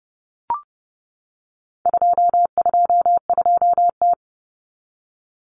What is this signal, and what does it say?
Morse code, heard as a keyed tone, about 30 wpm, 700 Hz. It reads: 222T
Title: Common Cut Numbers - 4-Digit 30wpm